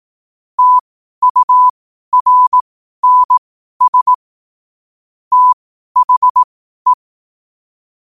Morse code, heard as a keyed tone, about 18 wpm, spaced as for 12 wpm, 1000 Hz. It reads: TURNS THE